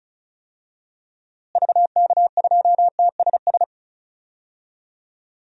Morse code, heard as a keyed tone, about 36 wpm, 700 Hz. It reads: VK2TSS
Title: DXCC-100 Call Signs - 6 Characters - Repeated 2x - 35wpm